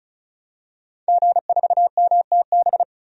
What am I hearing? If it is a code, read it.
Morse code, heard as a keyed tone, about 34 wpm, 700 Hz. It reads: G4MTB
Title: DXCC-100 Call Signs - 5 Characters - 35wpm